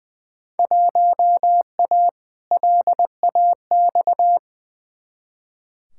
Morse code, heard as a keyed tone, about 20 wpm, 700 Hz. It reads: 1A LAX
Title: Field Day Exchanges - 20wpm